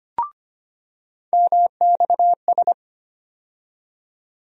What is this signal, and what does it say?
Morse code, heard as a keyed tone, about 25 wpm, 700 Hz. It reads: MXS
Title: USA Call Sign Postfixes - 3 Letters 25wpm